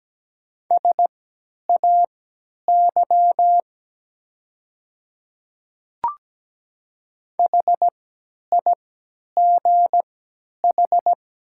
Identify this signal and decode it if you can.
Morse code, heard as a keyed tone, about 17 wpm, spaced as for 10 wpm, 690 Hz. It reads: SAY HIGH
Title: 3x Character Spacing - Top 300 Words - 17wpm